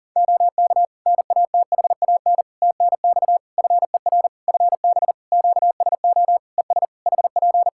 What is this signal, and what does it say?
Morse code, heard as a keyed tone, about 40 wpm, 700 Hz. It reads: OK NATHAN TNX FER FB QSO ES HP
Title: Practice QSO - Exchange 4 - Ending - 40wpm